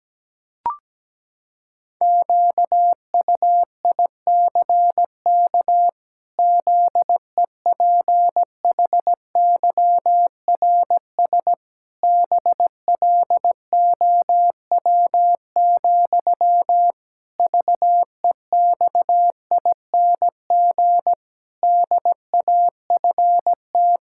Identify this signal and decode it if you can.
Morse code, heard as a keyed tone, about 17 wpm, 700 Hz. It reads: QUICK ZEPHYRS BLOW, VEXING DAFT
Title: Pangram Challenge v2 - 17wpm